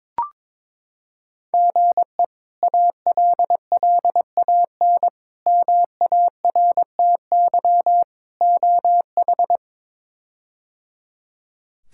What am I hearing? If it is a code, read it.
Morse code, heard as a keyed tone, about 22 wpm, 700 Hz. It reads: GE ALLAN MARTY OH
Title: K1USN SST QSO Practice - 22wpm